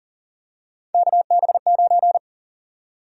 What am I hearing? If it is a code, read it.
Morse code, heard as a keyed tone, about 40 wpm, 700 Hz. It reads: KB9
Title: USA Call Sign Prefixes - Repeated 2x - 40wpm